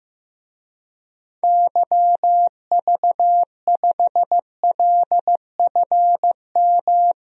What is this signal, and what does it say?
Morse code, heard as a keyed tone, about 15 wpm, 700 Hz. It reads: YV5LFM
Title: DXCC-100 Call Signs - 6 Characters - 15wpm